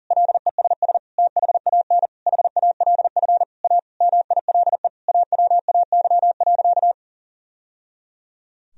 Morse code, heard as a keyed tone, about 37 wpm, 700 Hz. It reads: LESS THAN HALF A MILE AWAY.
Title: Sentences from Top 1000 Words - Review Part 2 - 40wpm